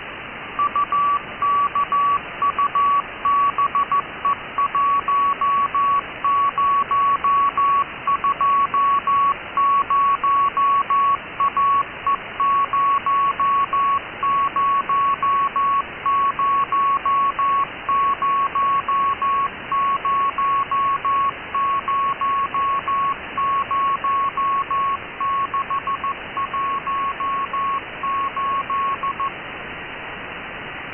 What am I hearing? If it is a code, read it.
Morse code, heard as a keyed tone, about 14 wpm, 1120 Hz, with noise. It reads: UKUBE1020AE0000000618
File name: ukube cw 12dec2014 2130z